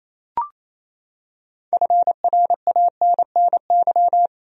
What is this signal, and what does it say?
Morse code, heard as a keyed tone, about 28 wpm, 700 Hz. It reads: FRANNY
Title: USA Common Names - 28wpm